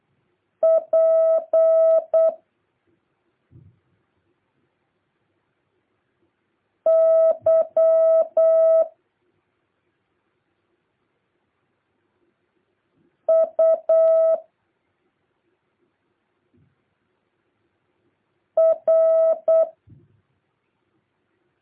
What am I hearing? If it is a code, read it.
Morse code, heard as a keyed tone, about 8 wpm, 650 Hz. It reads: P Y U R
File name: jh8cbh / 2023cw